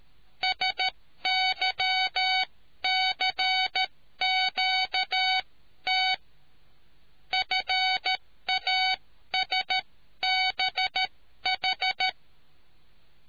Morse code, heard as a keyed tone, about 13 wpm, spaced as for 11 wpm, 750 Hz. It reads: SYCQT FASBH